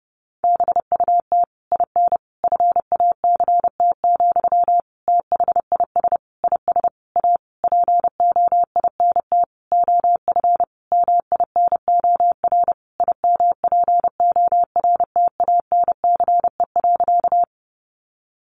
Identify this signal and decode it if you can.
Morse code, heard as a keyed tone, about 30 wpm, 700 Hz. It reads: BUT IN FACT, THIS IS A POINT OF MINOR IMPORTANCE.
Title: Sentences from Top 1500 Words 30wpm